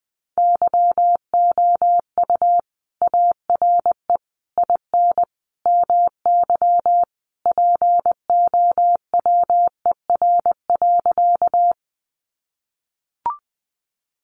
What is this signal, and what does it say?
Morse code, heard as a keyed tone, about 20 wpm, 700 Hz. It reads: YOU ARE IN MY POWER.
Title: Sentences from Top 500 Words 20wpm